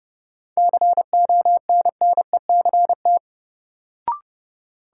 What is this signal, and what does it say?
Morse code, heard as a keyed tone, about 30 wpm, 700 Hz. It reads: CONNECT
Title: Sentences from Top 1500 Words - New Word Drill - 30wpm